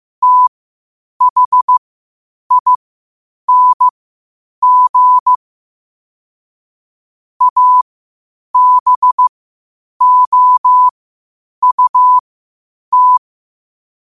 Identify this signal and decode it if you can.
Morse code, heard as a keyed tone, about 15 wpm, spaced as for 8 wpm, 1000 Hz. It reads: THING ABOUT
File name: qotdfiles / 8wpm